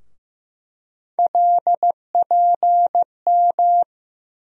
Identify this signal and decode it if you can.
Morse code, heard as a keyed tone, about 15 wpm, 700 Hz. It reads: LPM
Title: USA Call Sign Postfixes - 3 Letters 15wpm